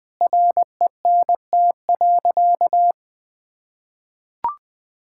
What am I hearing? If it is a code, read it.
Morse code, heard as a keyed tone, about 21 wpm, 700 Hz. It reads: RENT.
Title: Sentences from Top 1000 Words - Review Part 5 - 20wpm